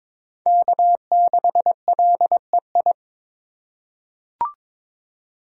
Morse code, heard as a keyed tone, about 21 wpm, 700 Hz. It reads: K6LEI
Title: USA Call Signs - 5 Characters - 2x Character Spacing - 22wpm